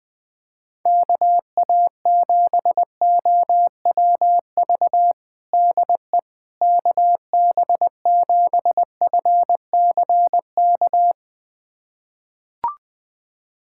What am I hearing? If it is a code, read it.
Morse code, heard as a keyed tone, about 20 wpm, 700 Hz. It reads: KA7OWV DE KB7FCK
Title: USA Callsign DE Callsign - 6 Characters - 20wpm